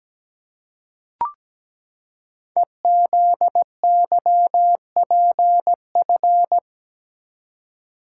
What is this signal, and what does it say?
Morse code, heard as a keyed tone, about 17 wpm, 700 Hz. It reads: EZYPF
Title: Code Groups - 5 Letters - 17wpm